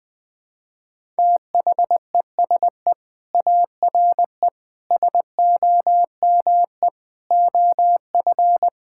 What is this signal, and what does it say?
Morse code, heard as a keyed tone, about 20 wpm, 700 Hz. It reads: THESE ARE SOME OF
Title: Sentences with 5-Word Phrases - 20wpm